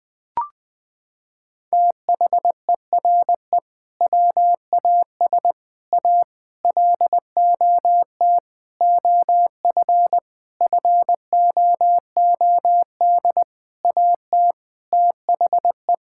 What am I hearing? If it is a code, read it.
Morse code, heard as a keyed tone, about 21 wpm, 700 Hz. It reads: THERE WAS A LOT OF FOOD AT THE
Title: Sentences with 5-Word Phrases - 20wpm